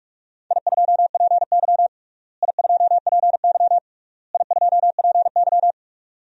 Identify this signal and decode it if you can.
Morse code, heard as a keyed tone, about 41 wpm, 700 Hz. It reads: I2PY I2PY I2PY
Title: DXCC-100 Call Signs - 4 Characters - Repeated 3x - 45wpm